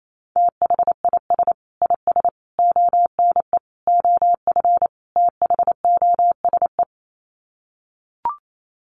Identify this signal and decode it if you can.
Morse code, heard as a keyed tone, about 28 wpm, 700 Hz. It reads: THIS IS ONE OF THOSE
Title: Sets of 5 Words Encore2 28wpm